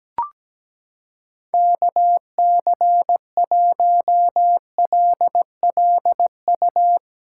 Morse code, heard as a keyed tone, about 17 wpm, 700 Hz. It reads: KC1LLU
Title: USA Callsign DE Callsign 17wpm